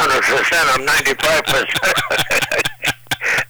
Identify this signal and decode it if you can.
Emergency Alert System (SAME) tones, heard as harsh buzzing bursts, with speech. It is heard from 0.2 to 3.3 s.